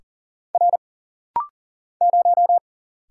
Morse code, heard as a keyed tone, about 40 wpm, 700 Hz. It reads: R 0
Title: Single Letter-Number Alt - Rapid-Fire 40wpm